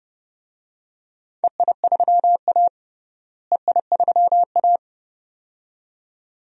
Morse code, heard as a keyed tone, about 30 wpm, 700 Hz. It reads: EI3A EI3A
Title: DXCC-100 Call Signs - 4 Characters - Repeated 2x - 30wpm